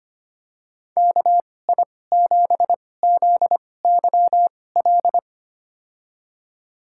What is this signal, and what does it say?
Morse code, heard as a keyed tone, about 25 wpm, spaced as for 18 wpm, 700 Hz. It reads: KI7ZYL
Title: USA Call Signs - 2x Character Spacing - 25wpm